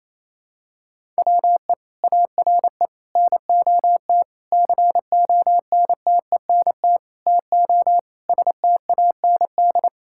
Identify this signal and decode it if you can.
Morse code, heard as a keyed tone, about 28 wpm, 700 Hz. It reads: WE ARE NOT CONTENT TO STAND
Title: Sentences from Top 1000 Words - Review Part 6 - 28wpm